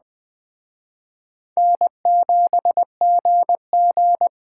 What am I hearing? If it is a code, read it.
Morse code, heard as a keyed tone, about 20 wpm, 700 Hz. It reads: N7GG